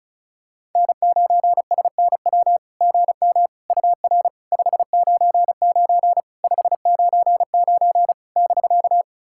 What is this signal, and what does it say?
Morse code, heard as a keyed tone, about 36 wpm, 700 Hz. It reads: N9SNW GM UR 599 599 <BK>
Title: SOTA QSO Practice 35wpm